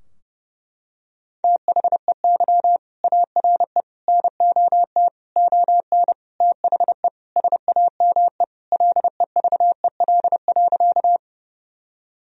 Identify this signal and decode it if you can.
Morse code, heard as a keyed tone, about 30 wpm, 700 Hz. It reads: THEY ARE NOT ON THE SAME LEVEL.
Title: Sentences from Top 1000 Words - Review Part 4 - 30wpm